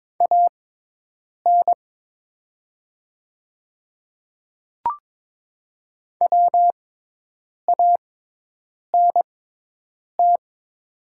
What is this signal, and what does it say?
Morse code, heard as a keyed tone, about 22 wpm, spaced as for 8 wpm, 700 Hz. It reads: AN WANT
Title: 6x Character Spacing - Top 200 Words - 22wpm